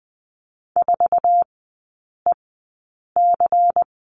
Morse code, heard as a keyed tone, about 20 wpm, spaced as for 8 wpm, 700 Hz. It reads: 4EC
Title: 152 Lesson 25e - Letter p - Three Characters 20wpm